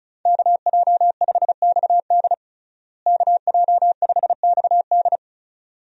Morse code, heard as a keyed tone, about 36 wpm, 700 Hz. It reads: KJ5XD KJ5XD
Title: USA Call Signs - 5 Characters - Repeated 2x - 35wpm